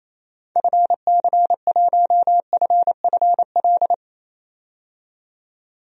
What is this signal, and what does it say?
Morse code, heard as a keyed tone, about 28 wpm, 700 Hz. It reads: FC1FFL
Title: DXCC-100 Call Signs - 6 Characters - 28wpm